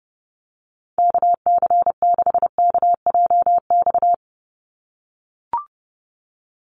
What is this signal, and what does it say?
Morse code, heard as a keyed tone, about 30 wpm, 700 Hz. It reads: KC6KJX
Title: USA Call Signs 30wpm